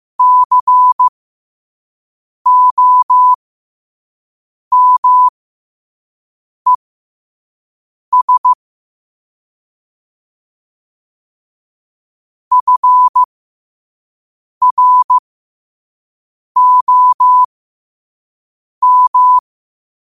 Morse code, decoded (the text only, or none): COMES FROM